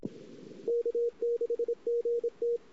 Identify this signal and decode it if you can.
Morse code, heard as a keyed tone, about 29 wpm, 470 Hz, with radio noise. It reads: K6GT
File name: CW / 2008-ss-cw-k5tr-op-n5rz -23:00